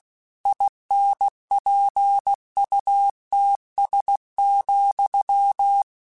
Morse code, heard as a keyed tone, about 16 wpm, 800 Hz. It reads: INPUTS,
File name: Technical Subjects / Technical Subjects-mp3s / AND Gate @15